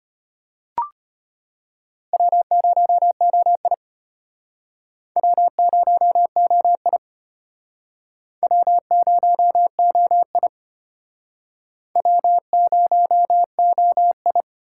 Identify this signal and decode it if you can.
Morse code, heard as a keyed tone, about 34 wpm, 700 Hz. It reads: W0OI W0OI W0OI W0OI